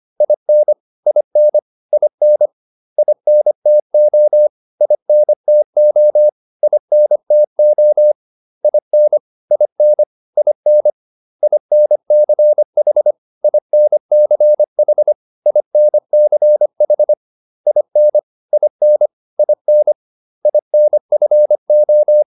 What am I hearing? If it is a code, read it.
Morse code, heard as a keyed tone, about 25 wpm, 600 Hz. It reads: IN IN IN INTO INTO INTO IN IN IN INCH INCH INCH IN IN IN INFO